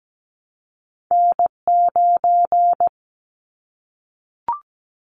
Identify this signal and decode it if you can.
Morse code, heard as a keyed tone, about 17 wpm, 700 Hz. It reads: N9
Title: Callsign Prefixes for Top 25 DXCC - 17wpm